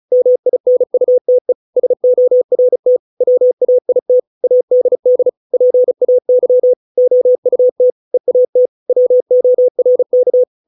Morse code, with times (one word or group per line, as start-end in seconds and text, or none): 0.1-1.5 MINUTE
1.8-3.0 SORT
3.2-4.2 WAIT
4.4-5.3 ADD
5.5-6.7 PAY
7.0-7.9 OUT
8.1-8.7 EAT
8.9-10.4 WORK